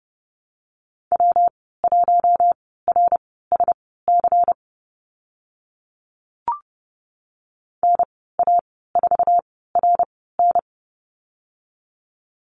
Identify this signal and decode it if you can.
Morse code, heard as a keyed tone, about 30 wpm, spaced as for 17 wpm, 700 Hz. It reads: W1RSC NA4RN
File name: Lesson 19g - Letter w - Call Signs 30wpm